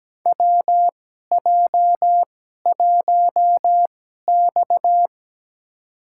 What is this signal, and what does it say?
Morse code, heard as a keyed tone, about 17 wpm, spaced as for 12 wpm, 700 Hz. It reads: WJ1X
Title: USA Call Signs - 4 Characters - 2x Character Spacing - 17wpm